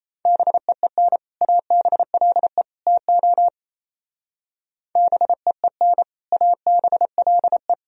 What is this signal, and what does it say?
Morse code, heard as a keyed tone, about 36 wpm, 700 Hz. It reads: BEEN ABLE TO BEEN ABLE